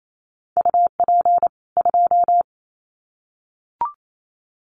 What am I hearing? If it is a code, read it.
Morse code, heard as a keyed tone, about 28 wpm, 700 Hz. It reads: UP 2
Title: Common QSO Phrases 28wpm